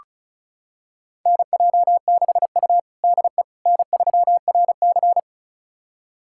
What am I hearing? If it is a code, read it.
Morse code, heard as a keyed tone, about 36 wpm, 700 Hz. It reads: NJ6U DE N3RC
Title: USA Callsign DE Callsign - 4 Characters - 35wpm